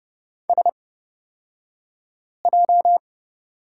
Morse code, heard as a keyed tone, about 31 wpm, spaced as for 5 wpm, 700 Hz. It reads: SJ
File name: Lesson 38d - Letter q - Two Characters 30wpm